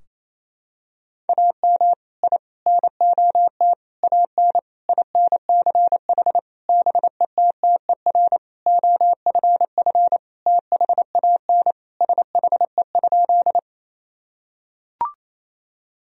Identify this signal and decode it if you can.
Morse code, heard as a keyed tone, about 27 wpm, 700 Hz. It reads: AM I NOT AN INCH BETTER OFF THAN SHE?
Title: Sentences from Top 1000 Words - Review Part 3 - 28wpm